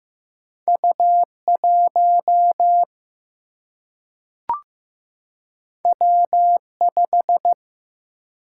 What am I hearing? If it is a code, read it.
Morse code, heard as a keyed tone, about 15 wpm, 700 Hz. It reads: U1 W5
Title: Single-Letter Single-Number 15wpm